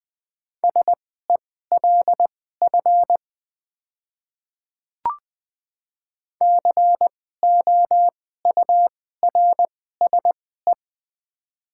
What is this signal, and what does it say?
Morse code, heard as a keyed tone, about 20 wpm, spaced as for 14 wpm, 700 Hz. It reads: SELF COURSE
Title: 2x Character Spacing - Top 500 Words - 20wpm